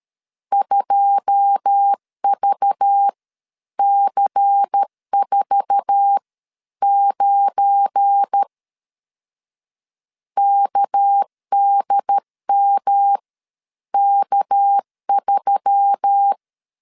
Morse code, heard as a keyed tone, about 13 wpm, spaced as for 10 wpm, 800 Hz. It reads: 2VC49 KDMK3
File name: CW / DET / 1 9 1500C S C 12wpm 3811